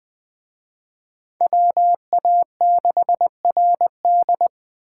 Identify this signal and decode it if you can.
Morse code, heard as a keyed tone, about 23 wpm, 700 Hz. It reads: WA6RD
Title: Speed Racing - USA 2x2 Call Signs 20wpm